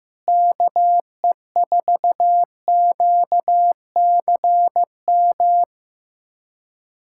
Morse code, heard as a keyed tone, about 16 wpm, 700 Hz. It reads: KE4QCM
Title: USA Callsign DE Callsign 15wpm